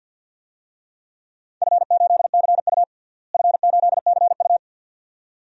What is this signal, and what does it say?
Morse code, heard as a keyed tone, about 47 wpm, 700 Hz. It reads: F8CU F8CU
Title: DXCC-100 Call Signs - 4 Characters - Repeated 2x - 50wpm